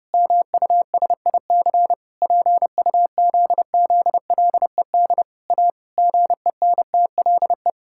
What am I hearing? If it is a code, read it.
Morse code, heard as a keyed tone, about 30 wpm, 700 Hz. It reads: MUSIC PUZZLED A GENTLE
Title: Pangram Challenge v2 - 30wpm